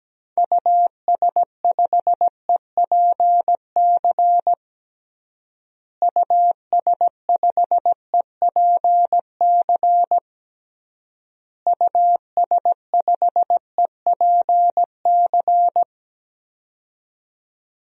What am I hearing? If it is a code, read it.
Morse code, heard as a keyed tone, about 17 wpm, 700 Hz. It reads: US5EPC US5EPC US5EPC
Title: DXCC-100 Call Signs - 6 Characters - Repeated 3x - 17wpm